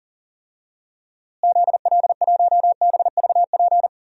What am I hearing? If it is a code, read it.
Morse code, heard as a keyed tone, about 40 wpm, 700 Hz. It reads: ZL1BVP
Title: DXCC-100 Call Signs - 6 Characters - Repeated 2x - 40wpm